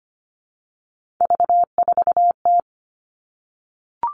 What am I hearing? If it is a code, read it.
Morse code, heard as a keyed tone, about 25 wpm, 700 Hz. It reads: V4T
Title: Three Challenging Letter-Number Combinations 25wpm